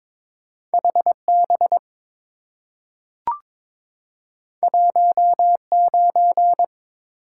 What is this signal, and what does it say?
Morse code, heard as a keyed tone, about 22 wpm, 700 Hz. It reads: HB 19
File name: practice / 22wpm / Two Letter-Number Combinations - 22wpm